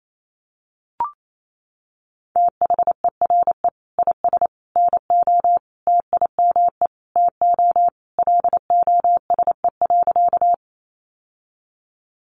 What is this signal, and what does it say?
Morse code, heard as a keyed tone, about 28 wpm, 700 Hz. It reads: THERE IS NO TIME TO LOSE.
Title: Sentences from Top 1100 Words 28wpm